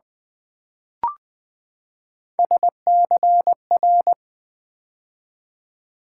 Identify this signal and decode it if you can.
Morse code, heard as a keyed tone, about 20 wpm, 700 Hz. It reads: SCR
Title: Build A Word - 3 Letters At A Time - 6-Letter Words - 20wpm